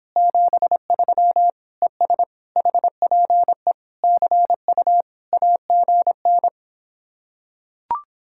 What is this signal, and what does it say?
Morse code, heard as a keyed tone, about 26 wpm, 700 Hz. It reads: 73 ES HPE CU AGN